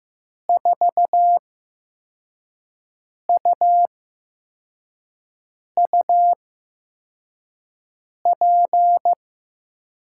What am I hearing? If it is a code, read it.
Morse code, heard as a keyed tone, about 15 wpm, spaced as for 4 wpm, 700 Hz. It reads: 4UUP
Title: USA Call Signs - 6 Characters - 8x Character Spacing - 15wpm